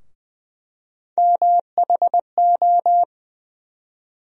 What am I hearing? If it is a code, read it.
Morse code, heard as a keyed tone, about 20 wpm, 700 Hz. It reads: MHO
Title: USA Call Sign Postfixes - 3 Letters 20wpm